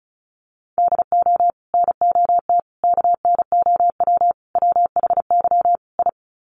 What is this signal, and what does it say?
Morse code, heard as a keyed tone, about 34 wpm, 700 Hz. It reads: DO NOT KNOW WHY I
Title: Sets of 5 Words Encore 35wpm